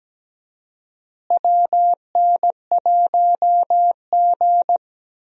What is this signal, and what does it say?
Morse code, heard as a keyed tone, about 17 wpm, 700 Hz. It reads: WN1G
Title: USA Call Signs - 4 Characters - Repeated 3x - 17wpm